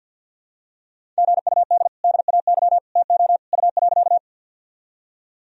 Morse code, heard as a keyed tone, about 52 wpm, 700 Hz. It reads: GUD DAY TO U.
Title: Sentences from Top 200 Words and Abbreviations 50wpm